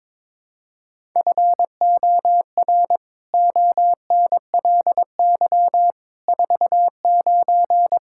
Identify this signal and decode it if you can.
Morse code, heard as a keyed tone, about 22 wpm, 700 Hz. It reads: FOR ONLY 49
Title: Pangram Challenge v2 - 22wpm